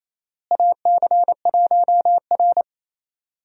Morse code, heard as a keyed tone, about 28 wpm, 700 Hz. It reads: AC1R
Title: USA Call Signs - 4 Characters - Repeated 3x - 28wpm